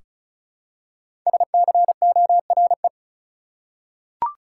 Morse code, heard as a keyed tone, about 34 wpm, 700 Hz. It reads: SCORE
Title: Five Letter Words - 35wpm